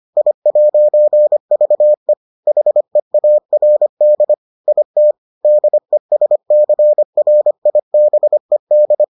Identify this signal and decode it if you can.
Morse code, heard as a keyed tone, about 25 wpm, 600 Hz. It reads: I'VE HEARD IT DESCRIBED